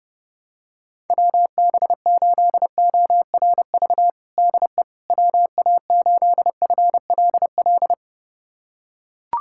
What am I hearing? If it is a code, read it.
Morse code, heard as a keyed tone, about 30 wpm, 700 Hz. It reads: WB8ORV DE WA8FLL
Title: USA Callsign DE Callsign - 6 Characters - 30wpm